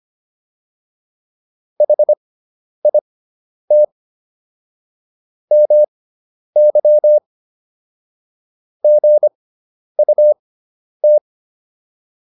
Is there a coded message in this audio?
Yes: Morse code from 1.8 to 11.2 s, heard as a keyed tone, about 25 wpm, spaced as for 10 wpm, 600 Hz. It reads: HIT MY GUT